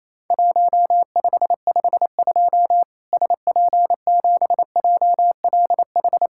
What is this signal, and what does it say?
Morse code, heard as a keyed tone, about 28 wpm, 700 Hz. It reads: 1552 SP7JLH
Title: Worked All Europe DX Contest - QTC Exchange - 28wpm